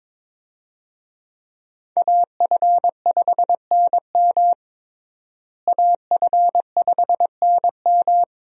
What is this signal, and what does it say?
Morse code, heard as a keyed tone, about 22 wpm, 700 Hz. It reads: AF5NM AF5NM
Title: USA Call Signs - 5 Characters - Repeated 2x - 22wpm